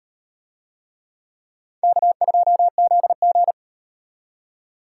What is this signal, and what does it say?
Morse code, heard as a keyed tone, about 38 wpm, 700 Hz. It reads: K2ZG